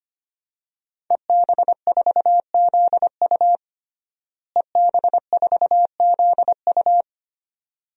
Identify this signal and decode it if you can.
Morse code, heard as a keyed tone, about 25 wpm, 700 Hz. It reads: EB4ZU EB4ZU
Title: DXCC-100 Call Signs - 5 Characters - Repeated 2x - 25wpm